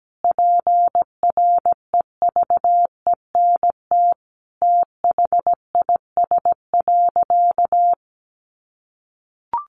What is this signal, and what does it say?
Morse code, heard as a keyed tone, about 17 wpm, 700 Hz. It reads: PREVENT THIS.
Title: Sentences from Top 1100 Words 17wpm